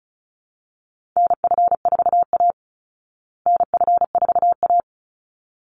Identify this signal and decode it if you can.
Morse code, heard as a keyed tone, about 34 wpm, 700 Hz. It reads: NF4A NF4A
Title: Call Signs Repeated 3x - 35wpm